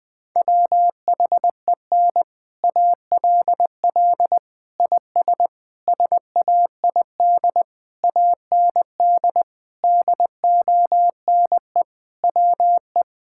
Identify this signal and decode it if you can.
Morse code, heard as a keyed tone, about 21 wpm, 700 Hz. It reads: WHEN ALL IS SAID AND DONE WE
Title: Sentences with 6-Word Phrases - Spoken Beginning - 20wpm